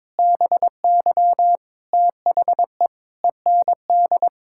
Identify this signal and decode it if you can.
Morse code, heard as a keyed tone, about 22 wpm, 700 Hz. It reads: BY THE END
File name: practice / 22wpm / Sentences with 8-Word Phrases - 22wpm